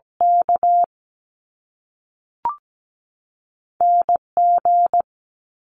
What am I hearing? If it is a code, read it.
Morse code, heard as a keyed tone, about 17 wpm, 700 Hz. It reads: K NG
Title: Top Two Letter Combinations in Words 17wpm